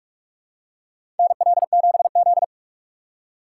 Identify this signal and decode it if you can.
Morse code, heard as a keyed tone, about 45 wpm, 700 Hz. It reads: NL7Z
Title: USA Call Signs - 4 Characters - 6x Character Spacing - 45wpm